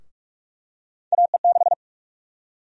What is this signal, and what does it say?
Morse code, heard as a keyed tone, about 46 wpm, 700 Hz. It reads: AE6
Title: USA Call Sign Prefixes - Repeated 2x - 45wpm